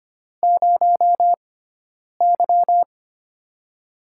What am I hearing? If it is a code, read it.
Morse code, heard as a keyed tone, about 25 wpm, spaced as for 9 wpm, 700 Hz. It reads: 0Y
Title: USA Call Signs - 4 Characters - 6x Character Spacing - 25wpm